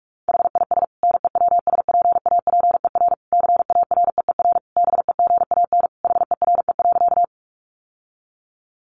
Morse code, heard as a keyed tone, about 44 wpm, 700 Hz. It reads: HIS NEWSPAPER CAREER BEGAN HERE.
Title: Sentences from Top 1100 Words 45wpm